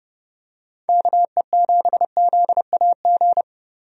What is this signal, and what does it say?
Morse code, heard as a keyed tone, about 30 wpm, 700 Hz. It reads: KE7ZAG
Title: USA Call Signs - 6 Characters - Repeated 3x - 30wpm